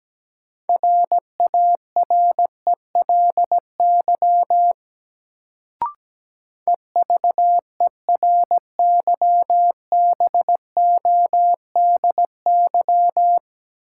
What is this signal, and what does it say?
Morse code, heard as a keyed tone, about 17 wpm, 700 Hz. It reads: RARELY EVERYBODY
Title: Sentences from Top 1400 Words - New Word Drill Alt - 17wpm